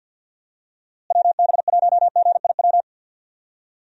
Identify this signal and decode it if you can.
Morse code, heard as a keyed tone, about 49 wpm, 700 Hz. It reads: WB1GIW
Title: USA Call Signs - 6 Characters - Repeated 3x - 50wpm